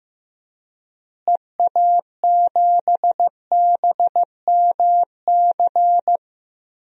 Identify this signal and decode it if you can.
Morse code, heard as a keyed tone, about 15 wpm, 700 Hz. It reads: EA7BMC
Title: DXCC-100 Call Signs - 6 Characters - Repeated 2x - 15wpm